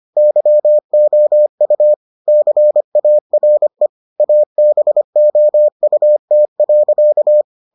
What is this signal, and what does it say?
Morse code, heard as a keyed tone, about 25 wpm, 600 Hz. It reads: YOU CARE ABOUT.